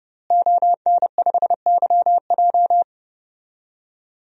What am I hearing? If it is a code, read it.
Morse code, heard as a keyed tone, about 30 wpm, 700 Hz. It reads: ON5YJ
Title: DXCC-100 Call Signs - 5 Characters - Repeated 3x - 30wpm